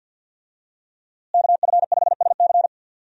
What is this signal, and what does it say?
Morse code, heard as a keyed tone, about 52 wpm, 700 Hz. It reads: KF5SC
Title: USA Call Signs - 5 Characters - Repeated 2x - 50wpm